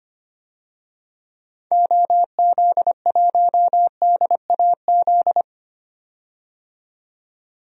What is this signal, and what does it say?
Morse code, heard as a keyed tone, about 25 wpm, 700 Hz. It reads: OZ1DAZ
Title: DXCC-100 Call Signs - 6 Characters - 25wpm